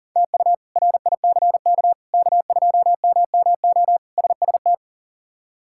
Morse code, heard as a keyed tone, about 37 wpm, 700 Hz. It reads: TU RICK K2MMO SST
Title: K1USN SST QSO Practice - 40wpm